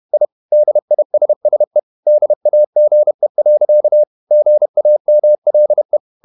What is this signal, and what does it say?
Morse code, heard as a keyed tone, about 31 wpm, 600 Hz. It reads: I DISSE DAGE. GAMLE
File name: Ferie QRQ 31wpm